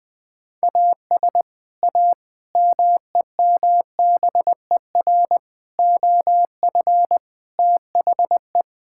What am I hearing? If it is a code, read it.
Morse code, heard as a keyed tone, about 21 wpm, 700 Hz. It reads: AS A MEMBER OF THE
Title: Build A Phrase - 5-Word Phrases - 20wpm